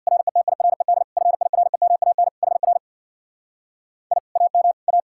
Morse code, heard as a keyed tone, about 63 wpm, 700 Hz. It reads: RETIRED FIREMAN HR I AM A